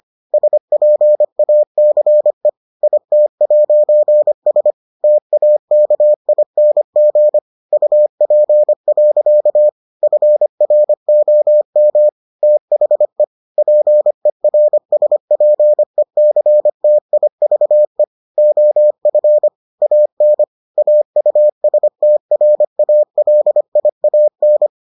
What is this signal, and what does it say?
Morse code, heard as a keyed tone, about 25 wpm, 600 Hz. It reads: SPACE IT'S TAKING UP. FROM THE PERSPECTIVE OF AN AUSTRALIAN